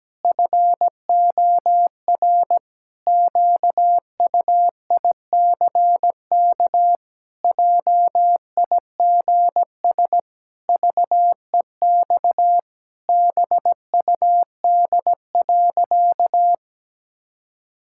Morse code, heard as a keyed tone, about 17 wpm, 700 Hz. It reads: FOR QUICK JIGS VEX BUD.
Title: Pangram Challenge v2 - 17wpm